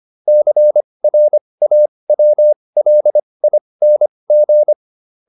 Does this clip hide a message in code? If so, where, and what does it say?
Yes: Morse code from 0.3 to 4.7 s, heard as a keyed tone, about 25 wpm, spaced as for 20 wpm, 600 Hz. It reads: CRAWLING